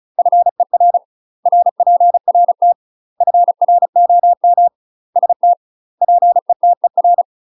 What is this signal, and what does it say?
Morse code, heard as a keyed tone, about 34 wpm, 700 Hz. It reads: FER RPRT FROM ST PETER
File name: UK MASTER QSOs TRAINING / 35WPM / UK MASTER QSOs 0004